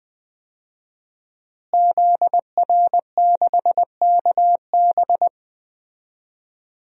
Morse code, heard as a keyed tone, about 20 wpm, 700 Hz. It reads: ZR6KB
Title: DXCC-100 Call Signs - 5 Characters - Repeated 2x - 20wpm